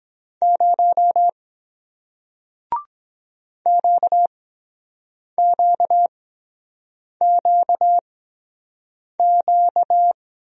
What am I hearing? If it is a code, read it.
Morse code, heard as a keyed tone, about 23 wpm, spaced as for 7 wpm, 700 Hz. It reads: 0 QQQQ